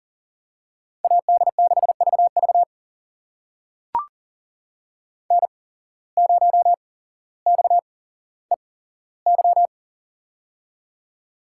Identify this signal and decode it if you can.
Morse code, heard as a keyed tone, about 40 wpm, spaced as for 17 wpm, 700 Hz. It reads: AD6VV N0XEY
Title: USA Call Signs - 5 Characters - 8x Character Spacing - 40wpm